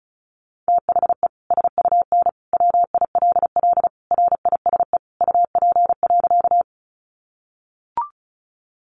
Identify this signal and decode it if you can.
Morse code, heard as a keyed tone, about 34 wpm, 700 Hz. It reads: THE SUN WILL RISE UP.
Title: Sentences from Top 900 Words 35wpm